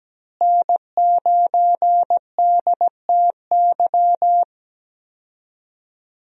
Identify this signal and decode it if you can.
Morse code, heard as a keyed tone, about 17 wpm, 700 Hz. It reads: N9DTY
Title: USA Call Signs - 5 Characters - Repeated 3x - 17wpm